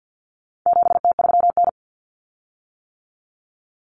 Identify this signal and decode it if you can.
Morse code, heard as a keyed tone, about 50 wpm, 700 Hz. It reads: 7T3N
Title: Common Cut Numbers - 4-Digit 50wpm